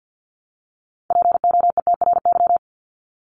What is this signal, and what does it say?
Morse code, heard as a keyed tone, about 48 wpm, 700 Hz. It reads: POETRY
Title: Top 1751-2000 Words 50wpm